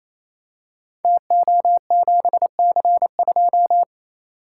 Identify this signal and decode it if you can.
Morse code, heard as a keyed tone, about 27 wpm, 700 Hz. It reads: TO7C2
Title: Code Groups - 5 Letter:Numbers - 28wpm